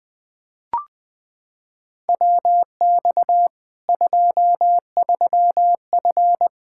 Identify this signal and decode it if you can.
Morse code, heard as a keyed tone, about 20 wpm, 700 Hz. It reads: WX 23F
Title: Practice QSO - Exchange 2 - Info - 20wpm